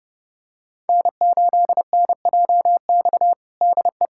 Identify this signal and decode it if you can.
Morse code, heard as a keyed tone, about 32 wpm, 700 Hz. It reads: N8NJX DE
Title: USA Callsign DE Callsign - 5 Characters - 30wpm